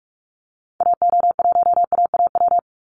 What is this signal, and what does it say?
Morse code, heard as a keyed tone, about 44 wpm, 700 Hz. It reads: AO1AAW
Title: Call Signs Repeated 3x - 45wpm